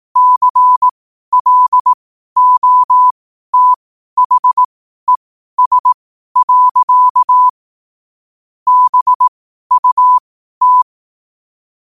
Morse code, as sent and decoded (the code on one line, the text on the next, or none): -.-. .-.. --- - .... ... .-.-.- / -... ..- -
CLOTHES. BUT